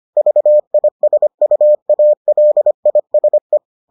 Morse code, heard as a keyed tone, about 25 wpm, 600 Hz. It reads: VISUALISE